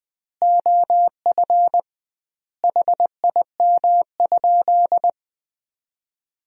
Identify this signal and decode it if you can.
Morse code, heard as a keyed tone, about 21 wpm, 700 Hz. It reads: OF HIM?
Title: 2x Word-Spacing - Sentences from Top 300 Words - 20wpm